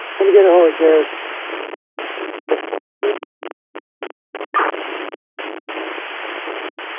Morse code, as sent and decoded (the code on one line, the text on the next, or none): none